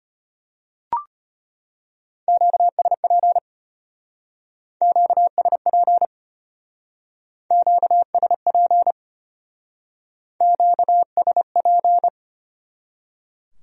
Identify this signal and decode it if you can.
Morse code, heard as a keyed tone, about 30 wpm, 700 Hz. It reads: QSP QSP QSP QSP
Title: Speed-Racing Q-Codes - 25wpm